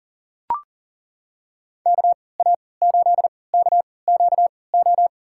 Morse code, heard as a keyed tone, about 39 wpm, spaced as for 22 wpm, 700 Hz. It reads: KA8KQO
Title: USA Call Signs - 6 Characters - 3x Character Spacing - 40wpm